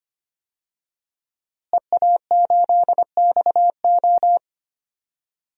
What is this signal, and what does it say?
Morse code, heard as a keyed tone, about 25 wpm, 700 Hz. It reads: EA8XO
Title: DXCC-100 Call Signs - 5 Characters - Repeated 3x - 25wpm